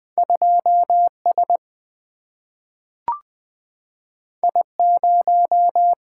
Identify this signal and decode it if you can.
Morse code, heard as a keyed tone, about 20 wpm, 700 Hz. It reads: 2 S I 0
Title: Single-Letter Single-Number 20wpm